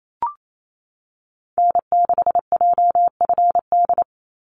Morse code, heard as a keyed tone, about 28 wpm, 700 Hz. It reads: N6JFD
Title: NA SOTA Activator Callsigns 28wpm